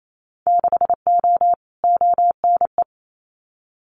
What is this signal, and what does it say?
Morse code, heard as a keyed tone, about 28 wpm, 700 Hz. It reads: 6O ONE
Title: Winter Field Day Exchanges Encore 28wpm